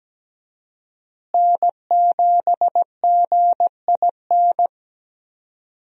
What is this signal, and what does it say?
Morse code, heard as a keyed tone, about 17 wpm, 700 Hz. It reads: N7GIN
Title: USA Call Signs - 5 Characters - Repeated 3x - 17wpm